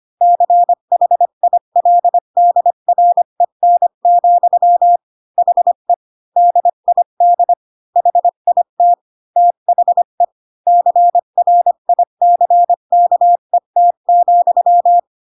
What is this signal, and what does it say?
Morse code, heard as a keyed tone, about 25 wpm, 700 Hz. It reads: CHILDREN, HE DID HIT THE CRICKET,
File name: Pinocchio / Pinocchio ch4 20 25 ik1hge